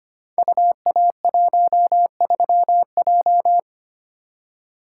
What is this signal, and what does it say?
Morse code, heard as a keyed tone, about 25 wpm, 700 Hz. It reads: UA13J
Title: Code Groups - 5 Letter:Numbers - 25wpm